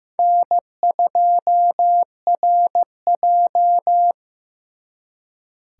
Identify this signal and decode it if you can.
Morse code, heard as a keyed tone, about 18 wpm, 700 Hz. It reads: N2RJ
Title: Speed Racing - USA 1x2 Call Signs 15wpm